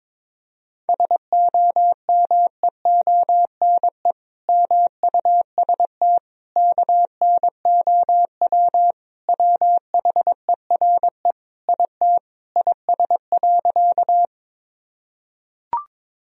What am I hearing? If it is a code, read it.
Morse code, heard as a keyed tone, about 22 wpm, 700 Hz. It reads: SOMEONE MUST KNOW WHERE IT IS.
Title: Sentences from Top 1000 Words - Review Part 4 - 22wpm